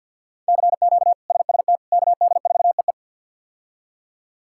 Morse code, heard as a keyed tone, about 52 wpm, 700 Hz. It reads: CQ SST KD4EE
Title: K1USN SST QSO Practice - 50wpm